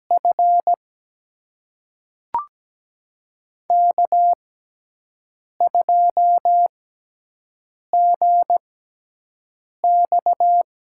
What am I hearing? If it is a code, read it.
Morse code, heard as a keyed tone, about 17 wpm, spaced as for 6 wpm, 700 Hz. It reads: F K2GX
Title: USA Call Signs - 6x Character Spacing - 17wpm